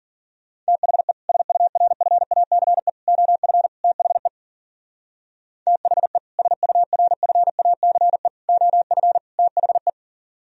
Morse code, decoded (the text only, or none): THE SURFACE OF THE THE SURFACE OF THE